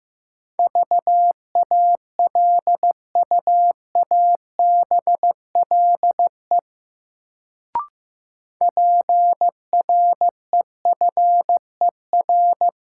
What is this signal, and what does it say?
Morse code, heard as a keyed tone, about 16 wpm, 700 Hz. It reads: VALUABLE PREFER
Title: Sentences from Top 1100 Words - New Word Drill Alt - 15wpm